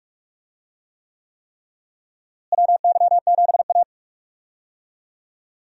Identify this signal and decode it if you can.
Morse code, heard as a keyed tone, about 46 wpm, 700 Hz. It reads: WY7A
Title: USA Call Signs - 4 Characters - Repeated 3x - 45wpm